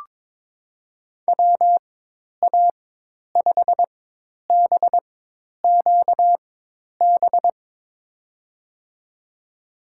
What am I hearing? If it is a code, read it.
Morse code, heard as a keyed tone, about 22 wpm, spaced as for 10 wpm, 700 Hz. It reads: WA5BQB
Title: USA Call Signs - 4x Character Spacing - 22wpm